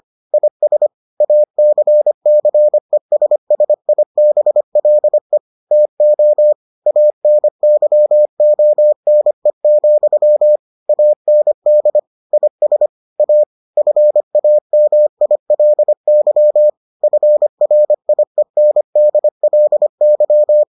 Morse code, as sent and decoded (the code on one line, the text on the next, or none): .. ... / .- -.-. -.-. ... ... .. -... .-.. / - --- / .- -. -.-- --- -. --..-- / .- -. -.. / .. ... / .- / ..-. .- -- .. .-.. -.-- / ..-. .-. .. -. -.. .-.. -.--
IS ACCESSIBLE TO ANYONE, AND IS A FAMILY FRIENDLY